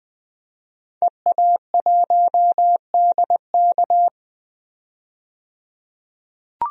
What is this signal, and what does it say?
Morse code, heard as a keyed tone, about 20 wpm, 700 Hz. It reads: EA1DK
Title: DXCC-100 Call Signs - 5 Characters - 20wpm